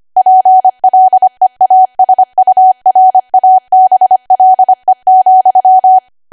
Morse code, heard as a keyed tone, about 25 wpm, 750 Hz. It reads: PLEASURABLE,
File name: W1AW-Level3/140218 25WPM